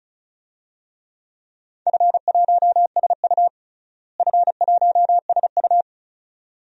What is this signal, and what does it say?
Morse code, heard as a keyed tone, about 34 wpm, 700 Hz. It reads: F1SU F1SU
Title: DXCC-100 Call Signs - 4 Characters - Repeated 2x - 35wpm